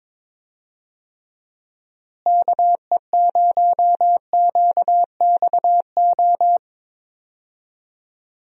Morse code, heard as a keyed tone, about 22 wpm, 700 Hz. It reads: KE0QXO
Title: USA Call Signs - 6 Characters - Repeated 3x - 22wpm